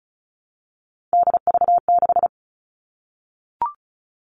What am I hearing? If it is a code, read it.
Morse code, heard as a keyed tone, about 34 wpm, 700 Hz. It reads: DV6
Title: All Cut Numbers - 3-Digit 35wpm